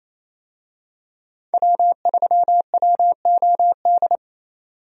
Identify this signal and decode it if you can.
Morse code, heard as a keyed tone, about 28 wpm, 700 Hz. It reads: W3WOD
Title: USA Call Signs - 5 Characters - 28wpm